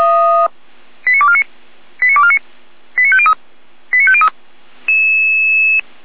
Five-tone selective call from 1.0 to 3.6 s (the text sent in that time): 89249 89249 89503